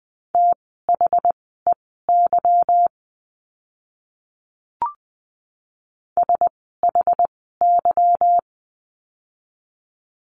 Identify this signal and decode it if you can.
Morse code, heard as a keyed tone, about 19 wpm, spaced as for 14 wpm, 700 Hz. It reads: THEY SHY
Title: 146 Lesson 24f - Letter y - Words 20wpm